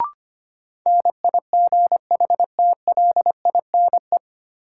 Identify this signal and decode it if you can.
Morse code, heard as a keyed tone, about 25 wpm, 700 Hz. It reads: NIGHTLINE
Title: Sets of 5 Words - New Word Drill - 25wpm